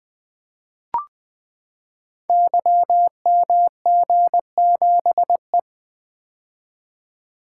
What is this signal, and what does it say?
Morse code, heard as a keyed tone, about 21 wpm, 700 Hz. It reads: YMG7E
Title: Code Groups - 5 Letter:Numbers - 20wpm